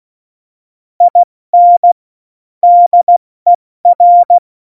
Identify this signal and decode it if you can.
Morse code, heard as a keyed tone, about 16 wpm, 700 Hz. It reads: IN DER
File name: Morsekurs / Witze / Witz 02. April 2019, 70 F80